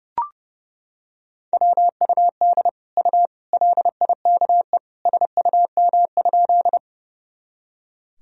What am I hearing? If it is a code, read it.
Morse code, heard as a keyed tone, about 32 wpm, 700 Hz. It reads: WUD U LIKE SUM?
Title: Sentences from Top 100 Words and Abbreviations 30wpm